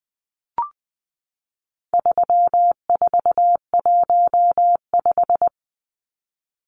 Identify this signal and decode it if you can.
Morse code, heard as a keyed tone, about 20 wpm, 700 Hz. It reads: 3415
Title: Numbers - 4-Digit 20wpm